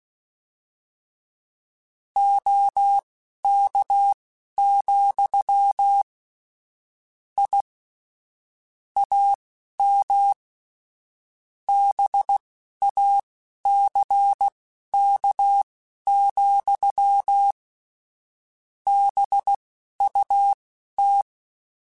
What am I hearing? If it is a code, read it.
Morse code, heard as a keyed tone, about 16 wpm, spaced as for 11 wpm, 800 Hz. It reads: OK, I AM BACK, BUT